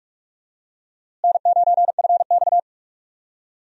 Morse code, heard as a keyed tone, about 41 wpm, 700 Hz. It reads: N9FX
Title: USA Call Signs - 4 Characters - Repeated 2x - 45wpm